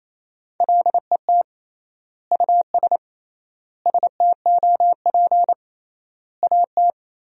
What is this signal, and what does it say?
Morse code, heard as a keyed tone, about 28 wpm, 700 Hz. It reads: LET US STOP AT
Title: 3x Word-Spacing - Sentences from Top 400 Words - 28wpm